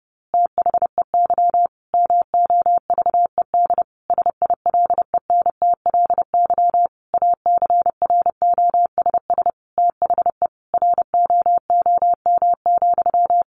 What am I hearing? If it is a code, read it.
Morse code, heard as a keyed tone, about 30 wpm, 700 Hz. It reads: THEY MOVED SILENTLY ACROSS THE ROOM,
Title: A Sound Of Thunder 30wpm